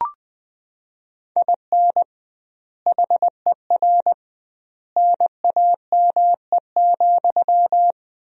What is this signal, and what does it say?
Morse code, heard as a keyed tone, about 20 wpm, 700 Hz. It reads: IN HER NAME,
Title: 2x Word-Spacing - Sentences from Top 500 Words - 20wpm